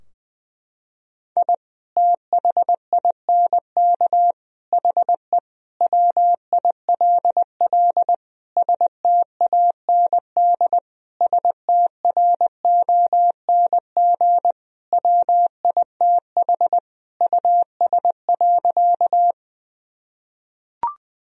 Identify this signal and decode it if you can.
Morse code, heard as a keyed tone, about 20 wpm, 700 Hz. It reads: I THINK HE WILL STAND STRONG WITH US.
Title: Sentences from Top 1000 Words - Review Part 3 - 20wpm